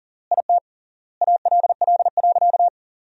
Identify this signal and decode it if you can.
Morse code, heard as a keyed tone, about 37 wpm, 700 Hz. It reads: IT ALL.
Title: 3x Word-Spacing - Sentences from Top 200 Words - 40wpm